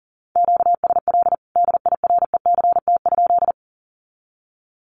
Morse code, heard as a keyed tone, about 40 wpm, 700 Hz. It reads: QSL DIRECT?
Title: Common QSO Phrases 40wpm